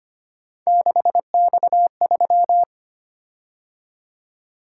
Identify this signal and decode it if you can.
Morse code, heard as a keyed tone, about 25 wpm, 700 Hz. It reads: 6X3
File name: Three Letter-Number Combinations - 25wpm